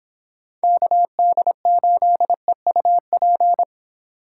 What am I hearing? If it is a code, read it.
Morse code, heard as a keyed tone, about 29 wpm, 700 Hz. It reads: KD8EUP